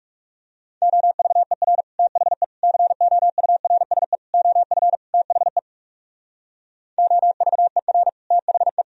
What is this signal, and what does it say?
Morse code, heard as a keyed tone, about 40 wpm, 700 Hz. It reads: OVER THE COURSE OF THE OVER THE